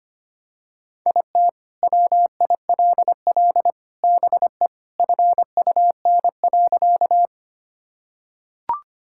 Morse code, heard as a keyed tone, about 25 wpm, 700 Hz. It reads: IT WILL BE FUN.
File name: Sentences with 3-Word Phrases - 25wpm